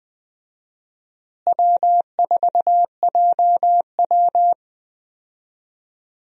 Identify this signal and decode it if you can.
Morse code, heard as a keyed tone, about 20 wpm, 700 Hz. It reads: W4JW
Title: USA Call Signs - 4 Characters - Repeated 3x - 20wpm